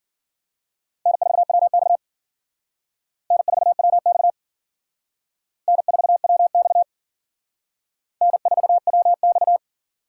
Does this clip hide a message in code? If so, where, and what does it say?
Yes: Morse code from 1.1 to 9.6 s, heard as a keyed tone, about 52 wpm, 700 Hz. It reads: N4WX N4WX N4WX N4WX